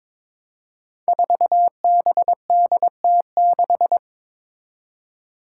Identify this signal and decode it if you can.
Morse code, heard as a keyed tone, about 22 wpm, 700 Hz. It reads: 4BDT6
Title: Five Challenging Letter-Number Combinations 22wpm